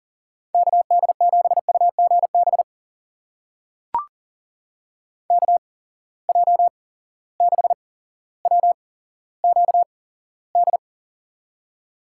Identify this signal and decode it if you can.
Morse code, heard as a keyed tone, about 40 wpm, spaced as for 17 wpm, 700 Hz. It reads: KD7UGB KJ6WQD
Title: USA Call Signs - 6 Characters - 8x Character Spacing - 40wpm